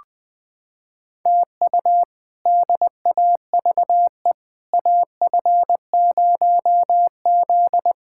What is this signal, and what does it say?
Morse code, heard as a keyed tone, about 20 wpm, 700 Hz. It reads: TU DAVE AF0Z
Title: K1USN SST QSO Practice - 20wpm